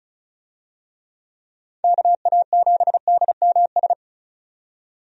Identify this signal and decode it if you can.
Morse code, heard as a keyed tone, about 36 wpm, 700 Hz. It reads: KA7DMS
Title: USA Call Signs - 6 Characters - 35wpm